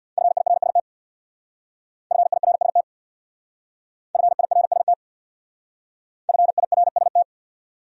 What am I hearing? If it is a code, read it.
Morse code, heard as a keyed tone, about 62 wpm, 700 Hz. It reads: FIRST FIRST FIRST FIRST
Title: Speed-Racing Top 100 Words 50wpm